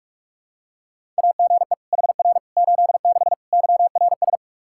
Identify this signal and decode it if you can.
Morse code, heard as a keyed tone, about 48 wpm, 700 Hz. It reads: AGE HR 86 YRS
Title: Practice QSO - Exchange 3 - More Info - 45wpm